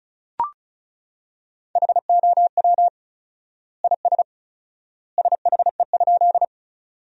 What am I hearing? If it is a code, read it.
Morse code, heard as a keyed tone, about 34 wpm, 700 Hz. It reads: HOW IS SHE?
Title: 4x Word-Spacing - Sentences from Top 100 Words - 35wpm